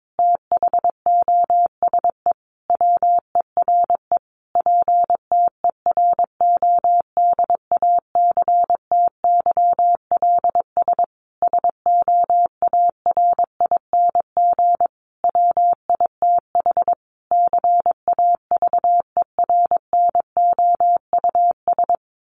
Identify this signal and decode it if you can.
Morse code, heard as a keyed tone, about 22 wpm, 700 Hz. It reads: THOSE WERE PTERODACTYLS SOARING WITH CAVERNOUS
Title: A Sound Of Thunder 22wpm